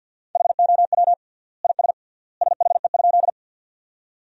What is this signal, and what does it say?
Morse code, heard as a keyed tone, about 50 wpm, 700 Hz. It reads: HOW IS SHE?
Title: 3x Word-Spacing - Sentences from Top 100 Words - 50wpm